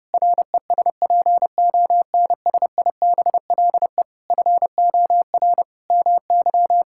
Morse code, heard as a keyed tone, about 30 wpm, 700 Hz. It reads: RESPONSIBLE FOR MY
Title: Sentences from Top 1600 Words 30wpm